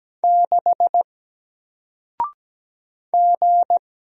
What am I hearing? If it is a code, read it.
Morse code, heard as a keyed tone, about 17 wpm, 700 Hz. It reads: 6 G
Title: Single Letter-Number 17wpm